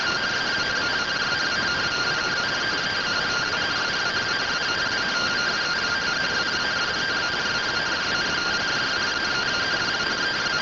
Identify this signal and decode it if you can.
Two-tone teleprinter signal, about 170 Hz shift.
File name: findings / SAGITAL nb203 fr61 3mm.gen-36 rtty rx